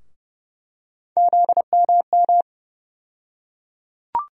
Morse code, heard as a keyed tone, about 31 wpm, 700 Hz. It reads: ZMM
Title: USA Call Sign Postfixes - 3 Letters 30wpm